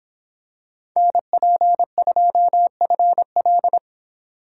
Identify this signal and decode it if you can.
Morse code, heard as a keyed tone, about 24 wpm, 700 Hz. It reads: NP2FL